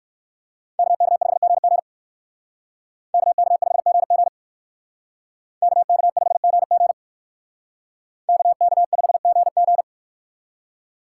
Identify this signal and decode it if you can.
Morse code, heard as a keyed tone, about 56 wpm, 700 Hz. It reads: KK5GG KK5GG KK5GG KK5GG
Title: Speed Racing - USA 2x2 Call Signs 45wpm